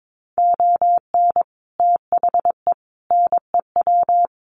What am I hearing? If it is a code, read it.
Morse code, heard as a keyed tone, about 22 wpm, 700 Hz. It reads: ON THE NEW
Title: Sets of 5 Words Encore 22wpm